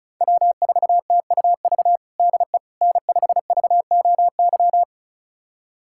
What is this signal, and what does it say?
Morse code, heard as a keyed tone, about 36 wpm, 690 Hz. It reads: W4TUV DE N5VOY
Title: USA Callsign DE Callsign - 5 Characters - 35wpm